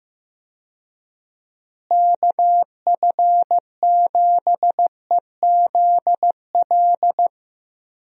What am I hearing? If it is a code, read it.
Morse code, heard as a keyed tone, about 15 wpm, 700 Hz. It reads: KF7EZL
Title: USA Call Signs - 6 Characters - Repeated 2x - 15wpm